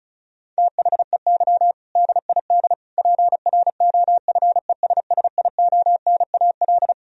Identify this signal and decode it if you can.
Morse code, heard as a keyed tone, about 36 wpm, 700 Hz. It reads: THEY DID PROFESSIONAL
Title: Sentences from Top 1000 Words - Review Part 4 - 35wpm